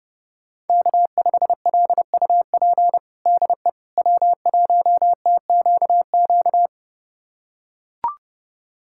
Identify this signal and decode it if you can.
Morse code, heard as a keyed tone, about 30 wpm, 700 Hz. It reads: K5LUP DE W1TQQ
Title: USA Callsign DE Callsign - 5 Characters - 30wpm